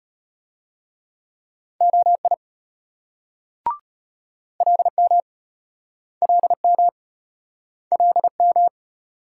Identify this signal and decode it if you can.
Morse code, heard as a keyed tone, about 33 wpm, 700 Hz. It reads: OI LM LM LM